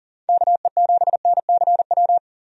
Morse code, heard as a keyed tone, about 40 wpm, 700 Hz. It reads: KE7NCW
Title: USA Call Signs - 6 Characters - 40wpm